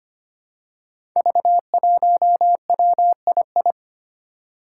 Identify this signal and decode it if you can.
Morse code, heard as a keyed tone, about 25 wpm, 700 Hz. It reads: V1WII
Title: Five Challenging Letter-Number Combinations 25wpm